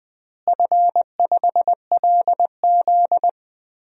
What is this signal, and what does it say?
Morse code, heard as a keyed tone, about 20 wpm, 700 Hz. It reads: F5LZ
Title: DXCC-100 Call Signs - 4 Characters - Repeated 2x - 20wpm